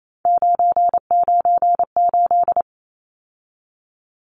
Morse code, heard as a keyed tone, about 28 wpm, 700 Hz. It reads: 998
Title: Numbers - 3-Digit 28wpm